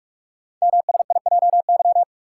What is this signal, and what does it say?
Morse code, heard as a keyed tone, about 44 wpm, 700 Hz. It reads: MSIJY
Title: Code Groups - 5 Letters - 45wpm